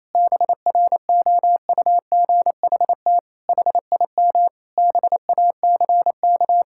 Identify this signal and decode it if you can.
Morse code, heard as a keyed tone, about 27 wpm, 700 Hz. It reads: BROUGHT HIM BACK
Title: Sentences from Top 1000 Words - Review Part 6 - 28wpm